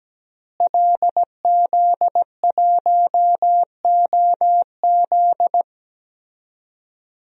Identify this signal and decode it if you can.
Morse code, heard as a keyed tone, about 17 wpm, 700 Hz. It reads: LZ1OZ
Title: DXCC-100 Call Signs - 5 Characters - Repeated 2x - 17wpm